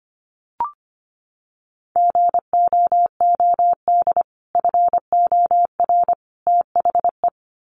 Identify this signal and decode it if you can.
Morse code, heard as a keyed tone, about 25 wpm, 700 Hz. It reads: GOOD FOR THE
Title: Sentences from Top 300 Words 25wpm